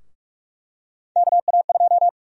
Morse code, heard as a keyed tone, about 45 wpm, 700 Hz. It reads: KA2
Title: USA Call Sign Prefixes - Repeated 2x - 45wpm